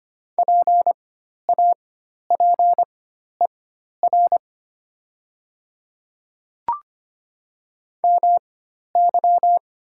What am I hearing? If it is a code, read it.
Morse code, heard as a keyed tone, about 25 wpm, spaced as for 12 wpm, 700 Hz. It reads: PAPER MY
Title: 4x Character Spacing - Top 300 Words - 25wpm